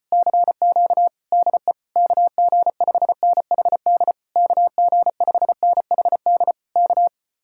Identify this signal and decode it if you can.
Morse code, heard as a keyed tone, about 33 wpm, 700 Hz. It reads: CQ DE KG5NHD KG5NHD K